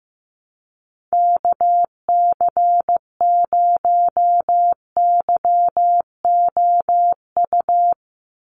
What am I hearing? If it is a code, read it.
Morse code, heard as a keyed tone, about 15 wpm, 700 Hz. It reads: KC0YOU
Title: USA Call Signs 15wpm